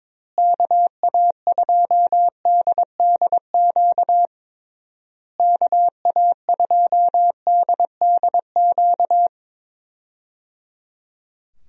Morse code, heard as a keyed tone, about 22 wpm, 700 Hz. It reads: KA2DDQ KA2DDQ
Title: USA Call Signs - 6 Characters - Repeated 3x - 22wpm